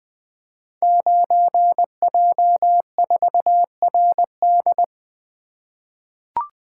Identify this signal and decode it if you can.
Morse code, heard as a keyed tone, about 20 wpm, 700 Hz. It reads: 9J4RD
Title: Code Groups - 5 Letter:Numbers - 20wpm